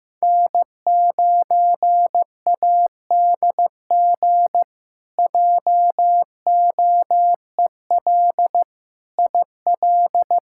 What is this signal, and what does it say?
Morse code, heard as a keyed tone, about 15 wpm, 700 Hz. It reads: N9ADG JOEL IL
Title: K1USN SST QSO Practice - 15wpm